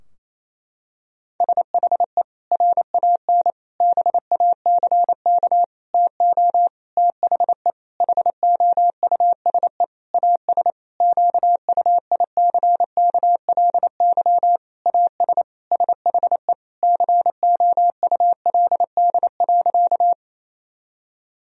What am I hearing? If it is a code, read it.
Morse code, heard as a keyed tone, about 27 wpm, 700 Hz. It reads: SHE RAN BACK TO THE HOUSE AS QUICKLY AS SHE COULD.
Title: Sentences from Top 1000 Words - Review Part 3 - 28wpm